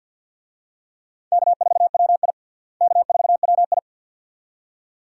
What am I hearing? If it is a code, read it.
Morse code, heard as a keyed tone, about 50 wpm, 700 Hz. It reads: K4WI K4WI
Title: USA Call Signs - 4 Characters - Repeated 2x - 50wpm